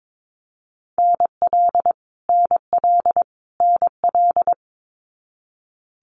Morse code, heard as a keyed tone, about 22 wpm, 700 Hz. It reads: NL NL NL
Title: Canadian Provinces and Territories - Repeated 3x - 22wpm